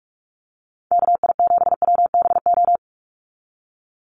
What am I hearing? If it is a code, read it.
Morse code, heard as a keyed tone, about 44 wpm, 700 Hz. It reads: KI7WBO
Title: USA Call Signs 45wpm